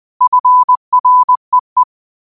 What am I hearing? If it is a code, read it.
Morse code, heard as a keyed tone, about 20 wpm, 1000 Hz. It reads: FREE